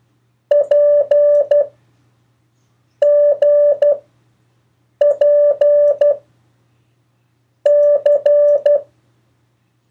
Morse code, heard as a keyed tone, about 12 wpm, 580 Hz. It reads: P G P C
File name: 1of4-cw-letters